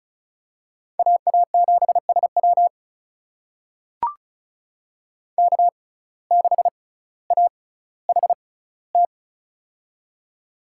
Morse code, heard as a keyed tone, about 36 wpm, spaced as for 18 wpm, 700 Hz. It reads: AA7SW K6AHT
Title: USA Call Signs - 5 Characters - 6x Character Spacing - 35wpm